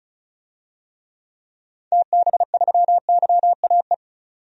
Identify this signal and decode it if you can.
Morse code, heard as a keyed tone, about 36 wpm, 700 Hz. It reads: TB3YAE
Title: DXCC-100 Call Signs - 6 Characters - 35wpm